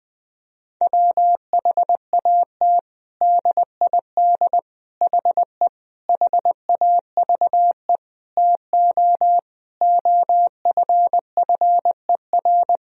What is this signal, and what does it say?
Morse code, heard as a keyed tone, about 19 wpm, 700 Hz. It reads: WHAT DID HE HAVE TO OFFER
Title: Sentences from Top 1000 Words - Review Part 4 - 20wpm